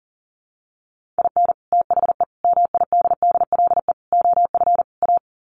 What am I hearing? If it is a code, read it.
Morse code, heard as a keyed tone, about 40 wpm, 700 Hz. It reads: IN THE MIDDLE OF A
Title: Sets of 5 Words 40wpm